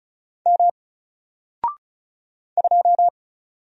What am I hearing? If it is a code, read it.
Morse code, heard as a keyed tone, about 36 wpm, 700 Hz. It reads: M 2
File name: Single-Letter-Number-35wpm